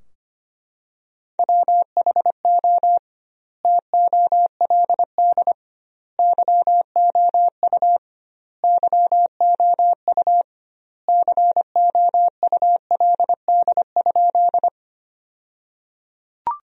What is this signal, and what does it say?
Morse code, heard as a keyed tone, about 25 wpm, 700 Hz. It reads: WHO TOLD YOU YOU COULD?
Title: 2x Word-Spacing - Sentences from Top 400 Words - 25wpm